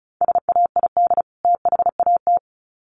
Morse code, heard as a keyed tone, about 34 wpm, 700 Hz. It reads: SAID THAT
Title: Sets of 2 Words 35wpm